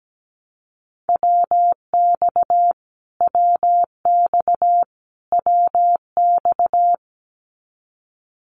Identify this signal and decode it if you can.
Morse code, heard as a keyed tone, about 17 wpm, 700 Hz. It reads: WX WX WX
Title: Abbreviations Repeated 3x - 17wpm